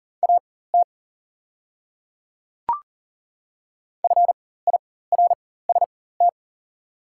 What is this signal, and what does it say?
Morse code, heard as a keyed tone, about 40 wpm, spaced as for 19 wpm, 690 Hz. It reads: AT FIRST
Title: 4x Character Spacing - Top 300 Words - 40wpm